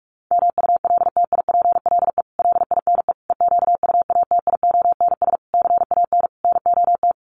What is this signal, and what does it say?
Morse code, heard as a keyed tone, about 44 wpm, 700 Hz. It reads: MULTIPLE LINE EQUATIONS CAN NOT
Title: Sets of 5 Words 45wpm